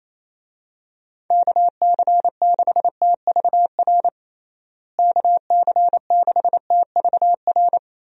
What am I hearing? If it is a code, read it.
Morse code, heard as a keyed tone, about 28 wpm, 700 Hz. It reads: KC6TVR KC6TVR
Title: USA Call Signs - 6 Characters - Repeated 3x - 28wpm